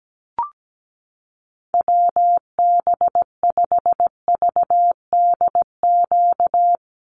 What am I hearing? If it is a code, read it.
Morse code, heard as a keyed tone, about 17 wpm, 700 Hz. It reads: WB5VDQ
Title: USA Call Signs 17wpm